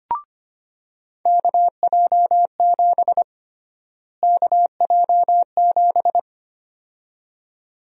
Morse code, heard as a keyed tone, about 25 wpm, 700 Hz. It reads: KJ7 KJ7
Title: USA Call Sign Prefixes - Repeated 2x - 25wpm